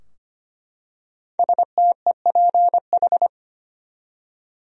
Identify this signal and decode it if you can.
Morse code, heard as a keyed tone, about 25 wpm, 700 Hz. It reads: STEPH
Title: USA Common Names - 25wpm